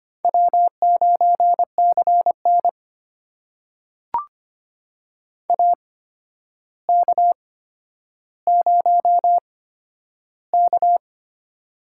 Morse code, heard as a keyed tone, about 26 wpm, spaced as for 11 wpm, 700 Hz. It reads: W9CN AK0K
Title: USA Call Signs - 4 Characters - 8x Character Spacing - 25wpm